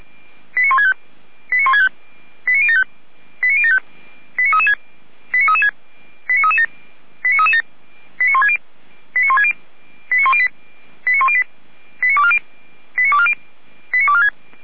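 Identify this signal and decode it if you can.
Five-tone selective call (ZVEI), heard as quick runs of five tones, tones of about 70 ms. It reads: 89165 89075 89075 89336 89336 89337 89337 89160 89160 89108 89108 89240 89246
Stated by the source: ZVEI five-tone call 89078 received by SDR.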